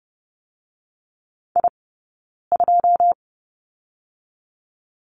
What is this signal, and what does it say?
Morse code, heard as a keyed tone, about 30 wpm, 700 Hz. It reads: I 2
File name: Lesson 18d - Letter m - Two Characters 30wpm